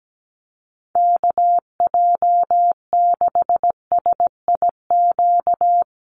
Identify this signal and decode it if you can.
Morse code, heard as a keyed tone, about 17 wpm, 700 Hz. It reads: KJ6SIQ
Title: USA Call Signs Repeated 2x 17wpm